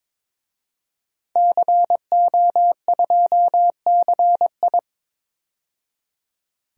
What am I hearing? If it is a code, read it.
Morse code, heard as a keyed tone, about 22 wpm, 700 Hz. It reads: CO2CI
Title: DXCC-100 Call Signs - 5 Characters - 22wpm